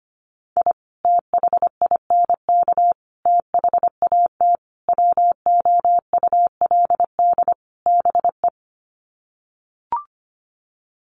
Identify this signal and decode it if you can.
Morse code, heard as a keyed tone, about 25 wpm, 700 Hz. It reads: I THINK THAT WOULD BE
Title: Sets of 5 Words Encore2 25wpm